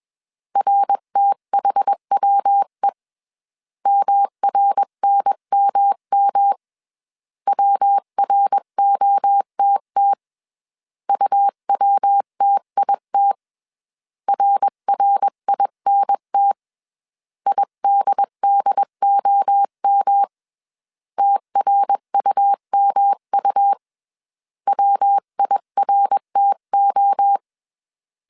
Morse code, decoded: RTHWE MRNMM WROTT UWTIT RRINT IDDOM TRUMU WIRTO

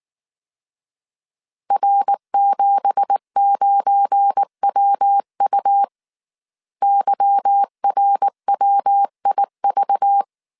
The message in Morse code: R79WU YRWIV

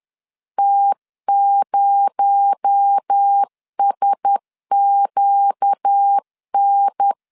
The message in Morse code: T0SQN

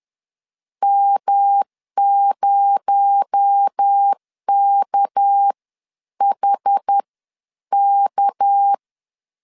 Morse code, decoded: M0KHK

